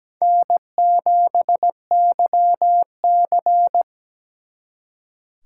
Morse code, heard as a keyed tone, about 20 wpm, 700 Hz. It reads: N7YC
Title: Speed Racing - USA Call Signs 17wpm